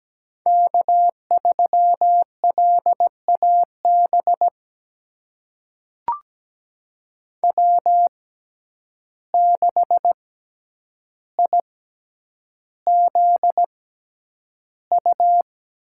Morse code, heard as a keyed tone, about 17 wpm, spaced as for 9 wpm, 700 Hz. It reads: K3LAB W6IZU
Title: USA Call Signs - 5 Characters - 6x Character Spacing - 17wpm